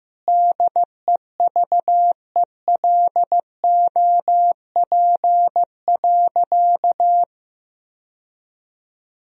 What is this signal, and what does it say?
Morse code, heard as a keyed tone, about 16 wpm, 700 Hz. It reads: DEVELOP.
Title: Sentences from Top 1000 Words - Review Part 2 - 15wpm